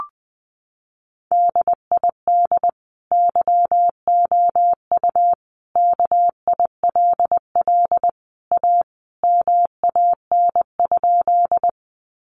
Morse code, heard as a keyed tone, about 20 wpm, 700 Hz. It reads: DID YOU KILL A MAN?
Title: Sentences from Top 1400 Words 20wpm